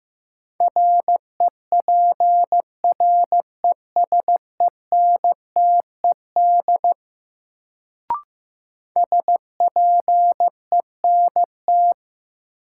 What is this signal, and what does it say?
Morse code, heard as a keyed tone, about 15 wpm, 700 Hz. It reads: REPRESENTED SPENT
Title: Sets of 5 Words - New Word Drill - 15wpm